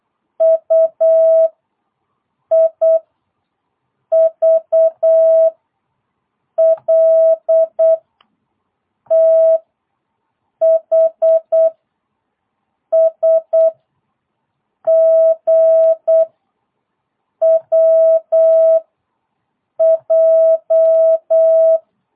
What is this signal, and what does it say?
Morse code, heard as a keyed tone, about 8 wpm, 650 Hz. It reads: U I V L T H S G W J